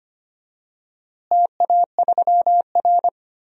Morse code, heard as a keyed tone, about 25 wpm, 700 Hz. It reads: TA3R
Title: DXCC-100 Call Signs - 4 Characters - Repeated 3x - 25wpm